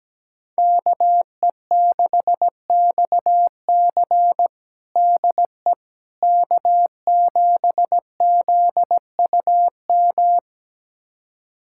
Morse code, heard as a keyed tone, about 17 wpm, 700 Hz. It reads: KE6XC DE K7ZUM
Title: USA Callsign DE Callsign - 5 Characters - 17wpm